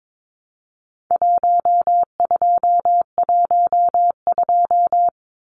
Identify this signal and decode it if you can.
Morse code, heard as a keyed tone, about 22 wpm, 700 Hz. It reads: 1212
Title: Numbers - Contest Style - 4-Digit 22wpm